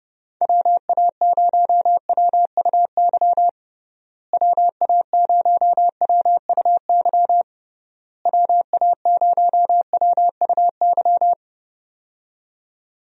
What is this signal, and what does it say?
Morse code, heard as a keyed tone, about 32 wpm, 700 Hz. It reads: WA0WUY WA0WUY WA0WUY
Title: USA Call Signs - 6 Characters - Repeated 3x - 30wpm